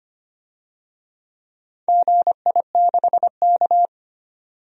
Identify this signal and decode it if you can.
Morse code, heard as a keyed tone, about 25 wpm, 700 Hz. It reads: GI6K
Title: DXCC-100 Call Signs - 4 Characters - 25wpm